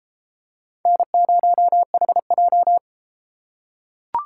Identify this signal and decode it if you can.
Morse code, heard as a keyed tone, about 29 wpm, 700 Hz. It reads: N0HJ